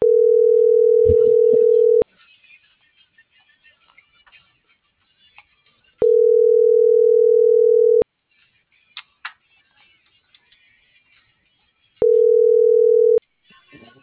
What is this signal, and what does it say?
Ringback tone (North America), 440 + 480 Hz.